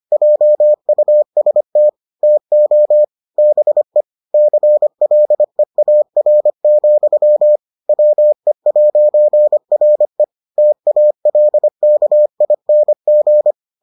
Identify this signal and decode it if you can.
Morse code, heard as a keyed tone, about 25 wpm, 600 Hz. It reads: JUST TO BE CLEAR, WE'RE TALKING